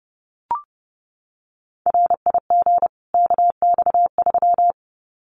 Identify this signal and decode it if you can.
Morse code, heard as a keyed tone, about 30 wpm, 700 Hz. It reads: RIG KX3
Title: Common QSO Phrases 30wpm